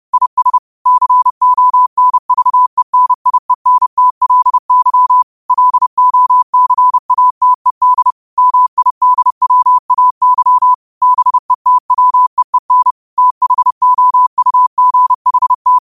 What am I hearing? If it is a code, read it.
Morse code, heard as a keyed tone, about 30 wpm, 1000 Hz. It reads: IS CONVENIENTLY LOCATED MIDWAY BETWEEN THOUGHT
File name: qotdfiles / 30wpm